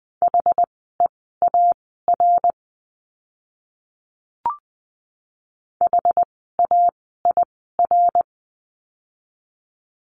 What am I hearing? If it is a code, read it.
Morse code, heard as a keyed tone, about 20 wpm, spaced as for 15 wpm, 700 Hz. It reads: HEAR HAIR
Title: 101 Lesson 17f - Letter u - Words 20wpm